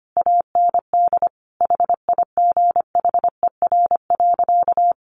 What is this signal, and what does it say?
Morse code, heard as a keyed tone, about 25 wpm, 700 Hz. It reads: AND HIGHER.
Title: Additional Easy Phrases 25wpm